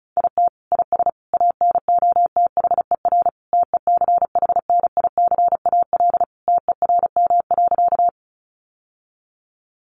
Morse code, heard as a keyed tone, about 34 wpm, 700 Hz. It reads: IT IS ANOTHER TECHNICAL TERM.
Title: Sentences from Top 1100 Words 35wpm